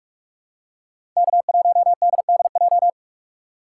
Morse code, heard as a keyed tone, about 44 wpm, 700 Hz. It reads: K1DDJ